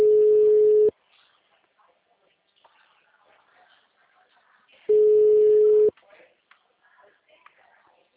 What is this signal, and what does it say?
Ringback tone (Europe, 425 Hz), with speech.
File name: out- -2021- - - .20